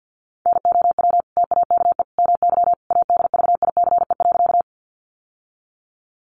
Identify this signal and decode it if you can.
Morse code, heard as a keyed tone, about 50 wpm, 700 Hz. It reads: NOW TAKE MY ADVICE.
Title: Sentences from Top 1100 Words 50wpm